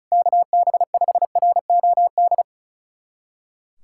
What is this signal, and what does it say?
Morse code, heard as a keyed tone, about 34 wpm, 700 Hz. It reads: KB5ROD
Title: USA Callsign DE Callsign 35wpm